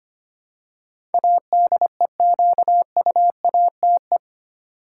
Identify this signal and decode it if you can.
Morse code, heard as a keyed tone, about 25 wpm, 700 Hz. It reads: ADEQUATE
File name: 8 Letter Words - First 3 Letters Given - 25wpm